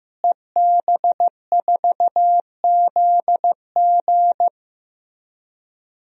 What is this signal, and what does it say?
Morse code, heard as a keyed tone, about 15 wpm, 700 Hz. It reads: EB4ZG
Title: DXCC-100 Call Signs - 5 Characters - Repeated 3x - 15wpm